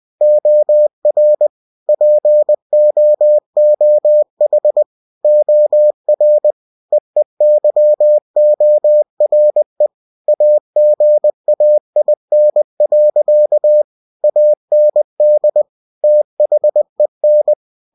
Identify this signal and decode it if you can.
Morse code, heard as a keyed tone, about 20 wpm, 600 Hz. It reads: OR POOH OR EEYORE AGAIN. AND THEN